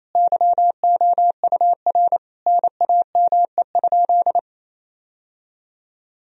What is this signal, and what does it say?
Morse code, heard as a keyed tone, about 28 wpm, 700 Hz. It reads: YOUR NAME?
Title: Sentences from Top 1000 Words - Review Part 3 - 28wpm